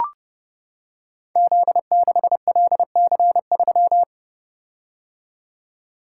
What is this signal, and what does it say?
Morse code, heard as a keyed tone, about 28 wpm, 700 Hz. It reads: Z6LC3
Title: Code Groups - 5 Letter:Numbers - 30wpm